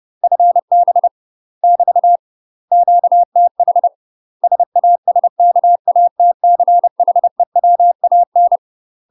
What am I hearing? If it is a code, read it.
Morse code, heard as a keyed tone, about 30 wpm, 700 Hz. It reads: FB <BT> QTH SASKATCHEWAN
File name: UK MASTER QSOs TRAINING / 30WPM / UK MASTER QSOs 0051